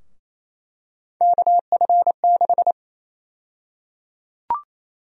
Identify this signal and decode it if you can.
Morse code, heard as a keyed tone, about 28 wpm, 700 Hz. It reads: KF6
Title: USA Call Sign Prefixes - Repeated 2x - 28wpm